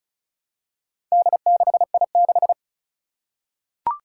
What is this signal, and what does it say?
Morse code, heard as a keyed tone, about 34 wpm, 700 Hz. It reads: D6I6
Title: Four Challenging Letter-Number Combinations 35wpm